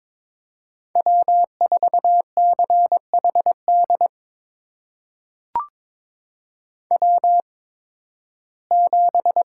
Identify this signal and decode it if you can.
Morse code, heard as a keyed tone, about 22 wpm, spaced as for 14 wpm, 700 Hz. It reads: W4CHD W7
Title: USA Call Signs - 5 Characters - 8x Character Spacing - 22wpm